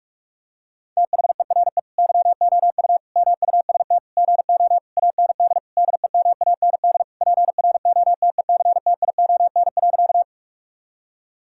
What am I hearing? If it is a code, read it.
Morse code, heard as a keyed tone, about 47 wpm, 700 Hz. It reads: THERE YOU MUST GO AND DEMAND PROTECTION.
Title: Sentences from Top 1600 Words 45wpm